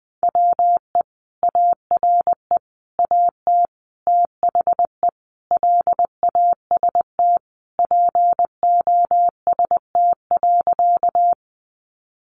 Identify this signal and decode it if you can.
Morse code, heard as a keyed tone, about 20 wpm, 700 Hz. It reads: WE ARE AT THE LAST POST.
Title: Sentences from Top 1000 Words 20wpm